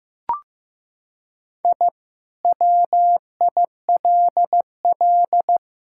Morse code, heard as a keyed tone, about 15 wpm, 700 Hz. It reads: I WILL
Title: Sentences from Top 1000 Words - Review Part 6 - 15wpm